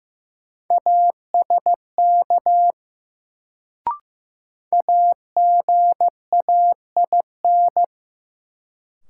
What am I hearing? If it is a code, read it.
Morse code, heard as a keyed tone, about 15 wpm, 700 Hz. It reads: ASK AGAIN
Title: Sentences from Top 200 Words - New Word Drill - 15wpm